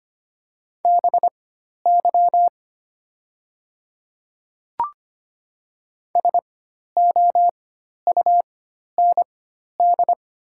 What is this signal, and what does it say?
Morse code, heard as a keyed tone, about 25 wpm, spaced as for 12 wpm, 700 Hz. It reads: BY SOUND
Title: 4x Character Spacing - Top 100 Words - 25wpm